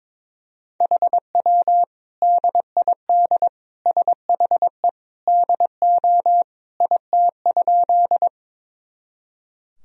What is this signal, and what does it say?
Morse code, heard as a keyed tone, about 22 wpm, 700 Hz. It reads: HW DID SHE DO IT?
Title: Sentences from Top 100 Words and Abbreviations 22wpm